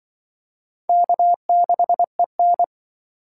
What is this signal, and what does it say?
Morse code, heard as a keyed tone, about 25 wpm, 690 Hz. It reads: K6EN